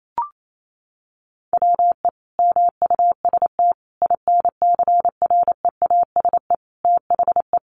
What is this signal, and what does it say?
Morse code, heard as a keyed tone, about 28 wpm, 700 Hz. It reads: WE MUST INCREASE THE
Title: Sentences from Top 1300 Words 28wpm